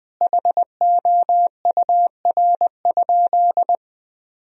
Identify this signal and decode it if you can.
Morse code, heard as a keyed tone, about 20 wpm, 700 Hz. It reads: HOUR?
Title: Sentences from Top 1000 Words - Review Part 4 - 20wpm